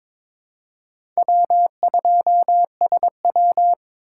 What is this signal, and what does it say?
Morse code, heard as a keyed tone, about 22 wpm, 700 Hz. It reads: W2SW
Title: USA Call Signs - 4 Characters - 2x Character Spacing - 22wpm